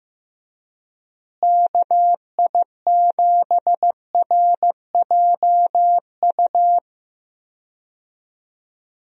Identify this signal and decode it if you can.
Morse code, heard as a keyed tone, about 15 wpm, 700 Hz. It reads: KI7RJU
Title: USA Call Signs - 6 Characters - Repeated 3x - 15wpm